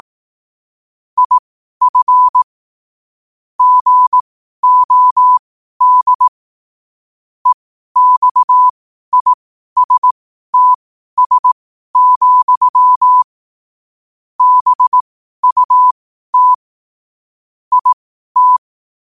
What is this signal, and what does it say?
Morse code, heard as a keyed tone, about 18 wpm, spaced as for 12 wpm, 1000 Hz. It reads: IF GOD EXISTS, BUT IT
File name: qotdfiles / 12wpm